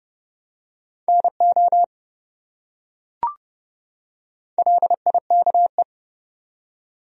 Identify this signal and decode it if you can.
Morse code, heard as a keyed tone, about 28 wpm, 700 Hz. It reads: NO LIKE